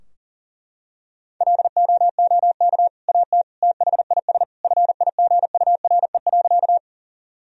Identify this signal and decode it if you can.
Morse code, heard as a keyed tone, about 40 wpm, 700 Hz. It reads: LOOK AT THIS FIGURE.
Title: Sentences from Top 1000 Words - Review Part 2 - 40wpm